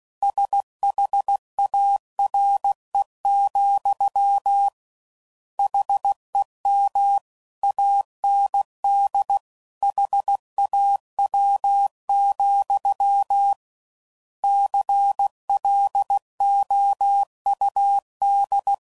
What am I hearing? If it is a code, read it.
Morse code, heard as a keyed tone, about 16 wpm, 800 Hz. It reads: SHARE, HEM AND HAW, CLOUD
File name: Mixed / Misc / WordPairs